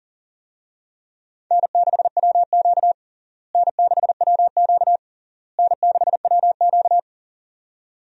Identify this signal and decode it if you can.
Morse code, heard as a keyed tone, about 40 wpm, 700 Hz. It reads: N6WQ N6WQ N6WQ
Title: USA Call Signs - 4 Characters - Repeated 3x - 40wpm